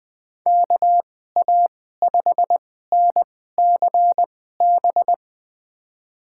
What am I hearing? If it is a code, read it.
Morse code, heard as a keyed tone, about 20 wpm, spaced as for 15 wpm, 700 Hz. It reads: KA5NCB
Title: USA Call Signs - 6 Characters - 2x Character Spacing - 20wpm